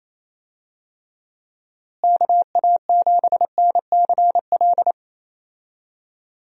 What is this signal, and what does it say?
Morse code, heard as a keyed tone, about 27 wpm, 700 Hz. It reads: KA7NCL
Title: USA Call Signs - 6 Characters - Repeated 3x - 28wpm